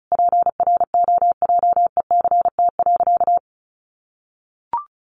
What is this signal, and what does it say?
Morse code, heard as a keyed tone, about 34 wpm, 700 Hz. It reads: PROJECT.
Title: Sentences from Top 1400 Words 35wpm